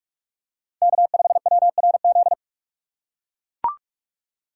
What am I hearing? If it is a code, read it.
Morse code, heard as a keyed tone, about 42 wpm, 700 Hz. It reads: K5WRZ
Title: USA Call Signs - 5 Characters - 4x Character Spacing - 45wpm